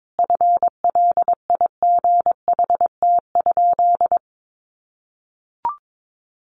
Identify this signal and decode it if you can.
Morse code, heard as a keyed tone, about 22 wpm, 700 Hz. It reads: FLIGHT?
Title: Sentences from Top 1000 Words 22wpm